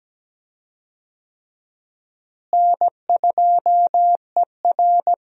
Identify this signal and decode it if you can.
Morse code, heard as a keyed tone, about 17 wpm, 700 Hz. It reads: N2ER
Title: USA Call Signs - 4 Characters - Repeated 3x - 17wpm